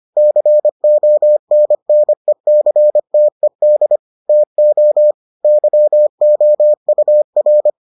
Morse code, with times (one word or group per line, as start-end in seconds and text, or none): 0.2-4.0 CONNECTED
4.3-5.1 TO
5.4-7.7 YOUR